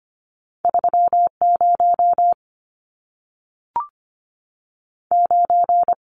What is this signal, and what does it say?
Morse code, heard as a keyed tone, about 25 wpm, 700 Hz. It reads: 30 9
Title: Numbers - 2-Digit 25wpm